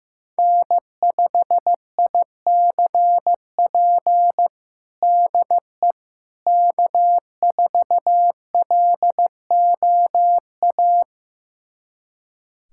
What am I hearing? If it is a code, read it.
Morse code, heard as a keyed tone, about 15 wpm, 700 Hz. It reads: N5ICP DE K4LOA
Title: USA Callsign DE Callsign - 5 Characters - 15wpm